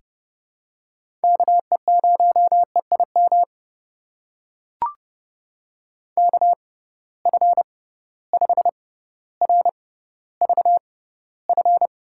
Morse code, heard as a keyed tone, about 30 wpm, spaced as for 15 wpm, 700 Hz. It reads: KE0EIM KF5RVF
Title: USA Call Signs - 6 Characters - 6x Character Spacing - 30wpm